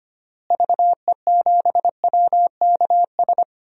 Morse code, heard as a keyed tone, about 25 wpm, 700 Hz. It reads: VE7WKS
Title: DXCC-100 Call Signs - 6 Characters - Repeated 3x - 25wpm